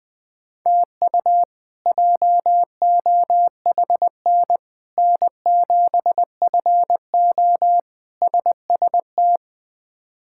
Morse code, heard as a keyed tone, about 20 wpm, 700 Hz. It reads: TU JOHN N7FO SST
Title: K1USN SST QSO Practice - 20wpm